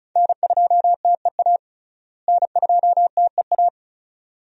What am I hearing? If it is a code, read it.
Morse code, heard as a keyed tone, about 36 wpm, 700 Hz. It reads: N2TEA N2TEA
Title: USA Call Signs - 5 Characters - Repeated 2x - 35wpm